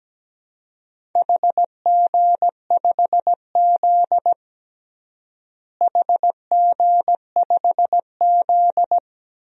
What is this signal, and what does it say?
Morse code, heard as a keyed tone, about 17 wpm, 700 Hz. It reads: HG5Z HG5Z
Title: DXCC-100 Call Signs - 4 Characters - Repeated 2x - 17wpm